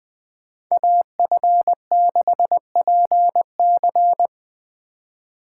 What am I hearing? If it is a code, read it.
Morse code, heard as a keyed tone, about 24 wpm, 700 Hz. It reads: AF6PC
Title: Speed Racing - USA 2x2 Call Signs 20wpm